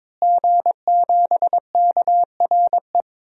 Morse code, heard as a keyed tone, about 22 wpm, 700 Hz. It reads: G7KRE
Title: DXCC-100 Call Signs - 5 Characters - Repeated 3x - 22wpm